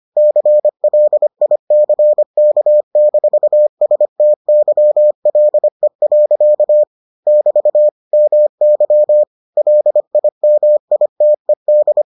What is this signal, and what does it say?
Morse code, heard as a keyed tone, about 25 wpm, 600 Hz. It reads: CLICK-STYLE. <BT> MY LIMITED